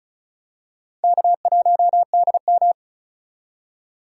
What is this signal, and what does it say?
Morse code, heard as a keyed tone, about 36 wpm, 700 Hz. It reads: K1DM
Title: K1USN SST QSO Practice - 35wpm